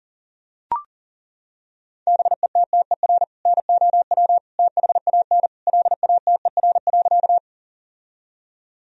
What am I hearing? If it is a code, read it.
Morse code, heard as a keyed tone, about 40 wpm, 700 Hz. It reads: BETTER NOW THAN LATER.
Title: Sentences from Top 1000 Words - Review Part 4 - 40wpm